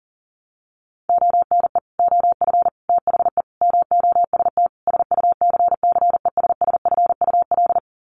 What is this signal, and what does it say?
Morse code, heard as a keyed tone, about 40 wpm, 700 Hz. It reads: ONE OF THE MOST SUCCESSFUL
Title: Sets of 5 Words Encore2 40wpm